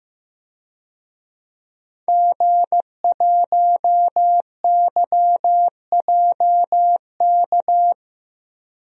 Morse code, heard as a keyed tone, about 15 wpm, 700 Hz. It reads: G1YJK
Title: DXCC-100 Call Signs - 5 Characters - Repeated 2x - 15wpm